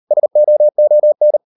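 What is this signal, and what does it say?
Morse code, heard as a keyed tone, about 38 wpm, 600 Hz. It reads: SOON
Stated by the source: Morse code practice at 38 WPM, sending Ken Follett's novel Never in English, from CW med Gnister.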